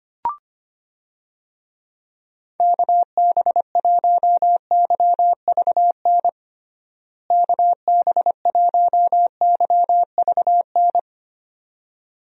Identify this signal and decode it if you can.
Morse code, heard as a keyed tone, about 25 wpm, 700 Hz. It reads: KB1YVN KB1YVN
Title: USA Call Signs - 6 Characters - Repeated 2x - 25wpm